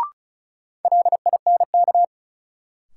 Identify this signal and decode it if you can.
Morse code, heard as a keyed tone, about 34 wpm, 700 Hz. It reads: LINK
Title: Sentences from Top 1000 Words - Review Part 7 - New Word Drill - 35wpm